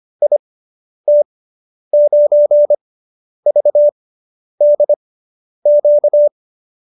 Morse code, heard as a keyed tone, about 25 wpm, spaced as for 10 wpm, 590 Hz. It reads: IT9VDQ